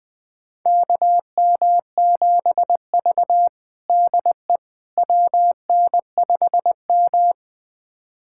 Morse code, heard as a keyed tone, about 20 wpm, 700 Hz. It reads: KM7V DE WN5M
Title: USA Callsign DE Callsign - 4 Characters - 20wpm